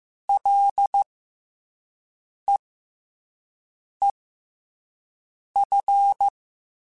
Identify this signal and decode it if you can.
Morse code, heard as a keyed tone, about 15 wpm, spaced as for 5 wpm, 800 Hz. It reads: LEEF
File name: F and L / F and L mp3 / F and L Prefixes 3x2 Twice @5